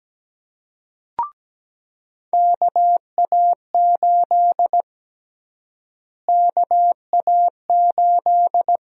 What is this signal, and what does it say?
Morse code, heard as a keyed tone, about 17 wpm, 700 Hz. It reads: KA8 KA8
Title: USA Call Sign Prefixes - Repeated 2x - 17wpm